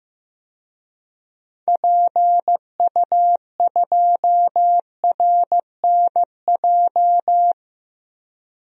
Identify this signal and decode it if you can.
Morse code, heard as a keyed tone, about 15 wpm, 700 Hz. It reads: PU2RNJ
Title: DXCC-100 Call Signs - 6 Characters - 15wpm